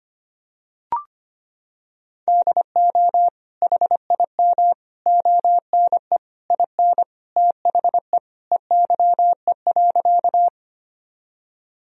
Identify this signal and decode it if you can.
Morse code, heard as a keyed tone, about 25 wpm, 700 Hz. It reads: DO HIM ONE IN THE EYE.
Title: Sentences from Top 1000 Words - Review Part 1 - 25wpm